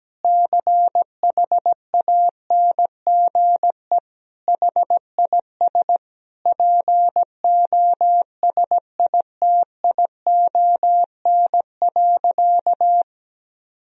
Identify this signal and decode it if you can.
Morse code, heard as a keyed tone, about 17 wpm, 700 Hz. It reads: CHANGE HIS POSITION.
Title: Sentences from Top 1000 Words - Review Part 1 - 17wpm